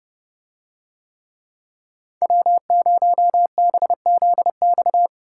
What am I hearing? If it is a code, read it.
Morse code, heard as a keyed tone, about 30 wpm, 700 Hz. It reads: W0BZX